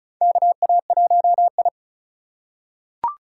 Morse code, heard as a keyed tone, about 34 wpm, 700 Hz. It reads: KA1I
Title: USA Call Signs - 2x Character Spacing - 35wpm